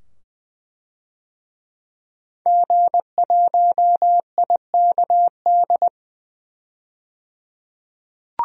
Morse code, heard as a keyed tone, about 20 wpm, 700 Hz. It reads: G1IKD
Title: DXCC-100 Call Signs - 5 Characters - 20wpm